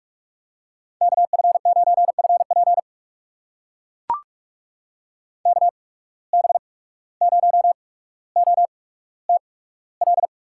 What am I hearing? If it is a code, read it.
Morse code, heard as a keyed tone, about 44 wpm, spaced as for 18 wpm, 700 Hz. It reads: KF9FP KB0OTL
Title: USA Call Signs - 8x Character Spacing - 45wpm